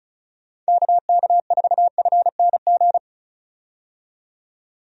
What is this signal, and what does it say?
Morse code, heard as a keyed tone, about 36 wpm, 700 Hz. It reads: KK4FNG
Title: USA Call Signs - 6 Characters - 35wpm